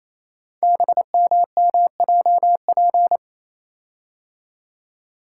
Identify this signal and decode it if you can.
Morse code, heard as a keyed tone, about 29 wpm, 700 Hz. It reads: BMMJP
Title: Code Groups - 5 Letter:Numbers - 28wpm